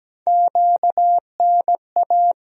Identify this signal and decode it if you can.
Morse code, heard as a keyed tone, about 17 wpm, 700 Hz. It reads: QNA
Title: USA Call Sign Postfixes - 3 Letters 17wpm